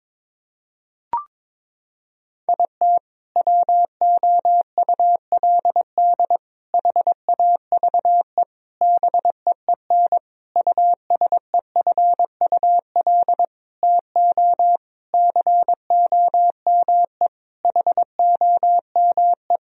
Morse code, heard as a keyed tone, about 22 wpm, 700 Hz. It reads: IT WOULD HAVE BEEN USEFUL TO COME HOME
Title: Sentences from Top 1000 Words - Review Part 4 - 22wpm